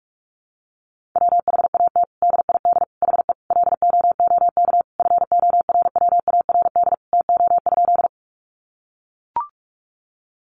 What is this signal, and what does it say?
Morse code, heard as a keyed tone, about 44 wpm, 700 Hz. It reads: WHAT DID HE LOOK FORWARD TO?
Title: Sentences from Top 1300 Words 45wpm